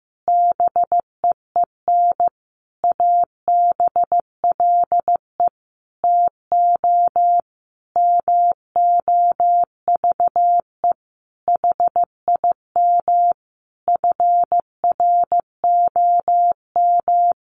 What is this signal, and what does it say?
Morse code, heard as a keyed tone, about 15 wpm, 700 Hz. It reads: BEEN ABLE TO MOVE HIM FROM
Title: Sentences from Top 1300 Words 15wpm